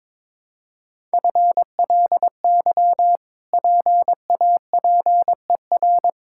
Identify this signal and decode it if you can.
Morse code, heard as a keyed tone, about 22 wpm, 700 Hz. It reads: FLY PAPER
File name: practice / 22wpm / Easy Sentences with Hints - 22wpm